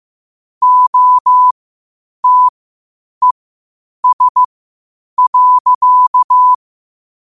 Morse code, heard as a keyed tone, about 15 wpm, spaced as for 8 wpm, 1000 Hz. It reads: OTES.